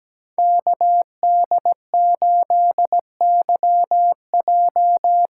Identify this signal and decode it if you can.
Morse code, heard as a keyed tone, about 17 wpm, 700 Hz. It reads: KD8YJ
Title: USA Call Signs - 6 Characters - 8x Character Spacing - 17wpm